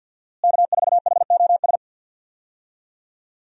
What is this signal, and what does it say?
Morse code, heard as a keyed tone, about 49 wpm, 700 Hz. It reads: K4HOS
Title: USA Call Signs - 5 Characters - Repeated 3x - 50wpm